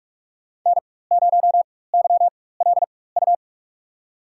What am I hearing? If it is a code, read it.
Morse code, heard as a keyed tone, about 46 wpm, spaced as for 21 wpm, 700 Hz. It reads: N0YLU
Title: USA Call Signs - 5 Characters - 4x Character Spacing - 45wpm